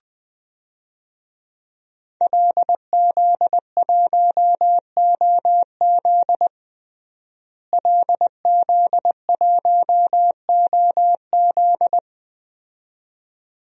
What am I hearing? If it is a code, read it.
Morse code, heard as a keyed tone, about 21 wpm, 700 Hz. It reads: LZ1OZ LZ1OZ
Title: DXCC-100 Call Signs - 5 Characters - Repeated 2x - 20wpm